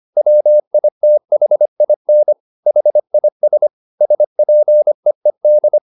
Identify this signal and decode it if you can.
Morse code, heard as a keyed tone, about 25 wpm, 600 Hz. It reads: WITHIN HIS SPEED
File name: the old man and the sea part 1 25wpm